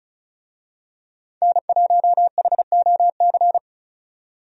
Morse code, heard as a keyed tone, about 34 wpm, 700 Hz. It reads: N1HOC
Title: USA Call Signs - 5 Characters - 35wpm